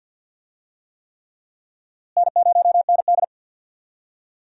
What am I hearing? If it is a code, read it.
Morse code, heard as a keyed tone, about 51 wpm, 700 Hz. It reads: N0ND
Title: USA Call Signs - 4 Characters - Repeated 3x - 50wpm